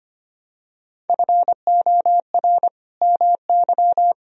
Morse code, heard as a keyed tone, about 25 wpm, 700 Hz. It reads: FOR MY
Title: Sentences from Top 1000 Words - Review Part 1 - 25wpm